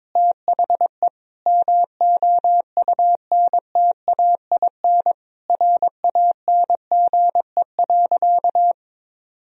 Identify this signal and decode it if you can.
Morse code, heard as a keyed tone, about 21 wpm, 700 Hz. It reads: THE MOUNTAIN RANGE.
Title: Sentences from Top 1000 Words - Review Part 4 - 22wpm